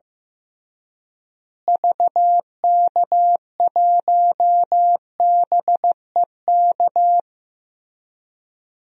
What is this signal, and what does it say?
Morse code, heard as a keyed tone, about 15 wpm, 700 Hz. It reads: VK1BEK
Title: DXCC-100 Call Signs - 6 Characters - Repeated 3x - 15wpm